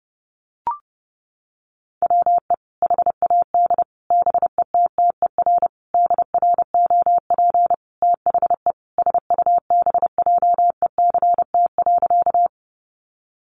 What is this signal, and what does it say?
Morse code, heard as a keyed tone, about 30 wpm, 700 Hz. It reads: WE HAD BETTER DROP THE SUBJECT.
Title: Sentences from Top 1400 Words 30wpm